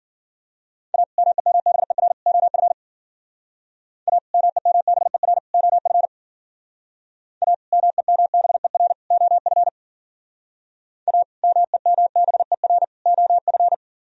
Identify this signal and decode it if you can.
Morse code, heard as a keyed tone, about 51 wpm, 700 Hz. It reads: A MEMBER OF A MEMBER OF A MEMBER OF A MEMBER OF